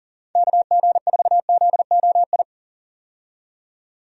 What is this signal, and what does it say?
Morse code, heard as a keyed tone, about 40 wpm, 700 Hz. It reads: KG4ZOI
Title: USA Call Signs - 6 Characters - 40wpm